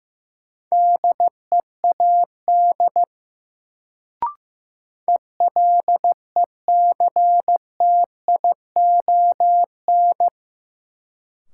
Morse code, heard as a keyed tone, about 16 wpm, 700 Hz. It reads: DEAD ELECTION
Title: Sentences from Top 1500 Words - New Word Drill Alt - 15wpm